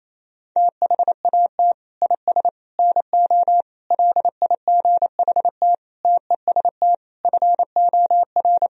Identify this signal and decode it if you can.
Morse code, heard as a keyed tone, about 28 wpm, 700 Hz. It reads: THAT IS NO LIGHT TEST FOR
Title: Sentences from Top 1000 Words - Review Part 3 - 28wpm